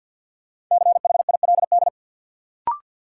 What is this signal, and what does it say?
Morse code, heard as a keyed tone, about 48 wpm, 700 Hz. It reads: CHILD
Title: Sets of 5 Words Encore - New Word Drill - 50wpm